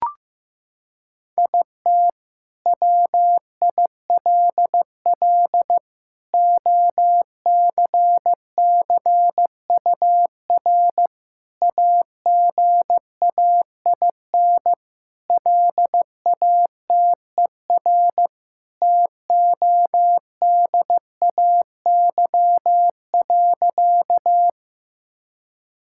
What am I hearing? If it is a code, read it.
Morse code, heard as a keyed tone, about 15 wpm, 700 Hz. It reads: IT WILL OCCUR AGAIN LATER TODAY.
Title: Sentences from Top 1000 Words - Review Part 6 - 15wpm